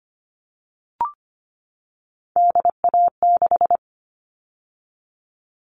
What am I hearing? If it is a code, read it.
Morse code, heard as a keyed tone, about 25 wpm, 700 Hz. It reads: DA6
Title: All Cut Numbers - 3-Digit 25wpm